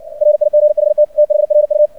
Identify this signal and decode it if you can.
Morse code, heard as a keyed tone, about 56 wpm, 600 Hz. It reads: RIGHT THRU
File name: archived / CRSnet